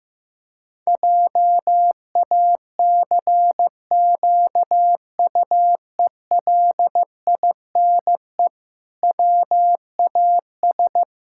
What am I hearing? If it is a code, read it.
Morse code, heard as a keyed tone, about 15 wpm, 700 Hz. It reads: JACQUELINE WAS
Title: Pangram Challenge v2 - 15wpm